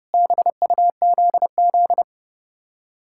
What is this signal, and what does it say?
Morse code, heard as a keyed tone, about 36 wpm, 700 Hz. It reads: BUZZ
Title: Speed-Racing Common QSO Phrases - 30wpm